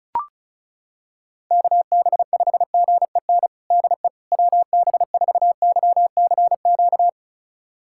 Morse code, heard as a keyed tone, about 34 wpm, 700 Hz. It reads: KB5GEN DE WB4YCQ
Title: USA Callsign DE Callsign - 6 Characters - 35wpm